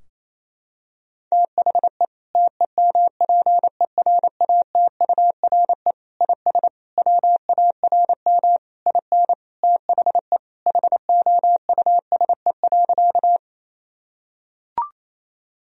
Morse code, heard as a keyed tone, about 28 wpm, 700 Hz. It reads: THE TEMPERATURE IS WARM IN THE HOUSE.
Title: Sentences from Top 1000 Words - Review Part 4 - 28wpm